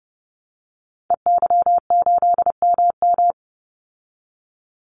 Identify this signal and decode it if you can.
Morse code, heard as a keyed tone, about 30 wpm, 700 Hz. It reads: EY8MM
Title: Call Signs Repeated 2x - 30wpm